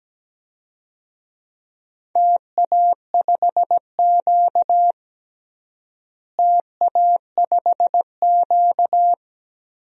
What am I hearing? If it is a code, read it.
Morse code, heard as a keyed tone, about 17 wpm, 700 Hz. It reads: TA5Q TA5Q
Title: DXCC-100 Call Signs - 4 Characters - Repeated 2x - 17wpm